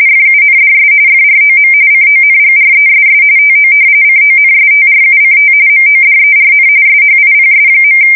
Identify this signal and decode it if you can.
RTTY, about 45.45 baud. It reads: FUNDS FROM CONGRESS TO CONSTRUCT MODERN SHIPS.
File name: fsk170 / usnavy3